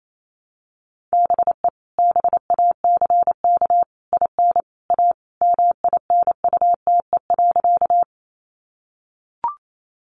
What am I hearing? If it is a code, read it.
Morse code, heard as a keyed tone, about 28 wpm, 700 Hz. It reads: BE BACK IN A MINUTE.
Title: Sentences from Top 500 Words 28wpm